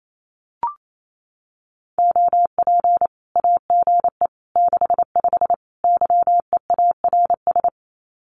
Morse code, heard as a keyed tone, about 28 wpm, 700 Hz. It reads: OP AGE 65 YEARS
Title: Common QSO Phrases 28wpm